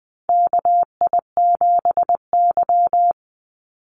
Morse code, heard as a keyed tone, about 20 wpm, 700 Hz. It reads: KI7Y
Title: Call Signs Repeated 2x - 20wpm